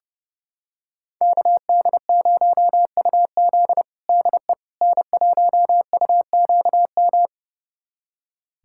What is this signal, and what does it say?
Morse code, heard as a keyed tone, about 30 wpm, 700 Hz. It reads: KD0UZ DE N1UQM
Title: USA Callsign DE Callsign - 5 Characters - 30wpm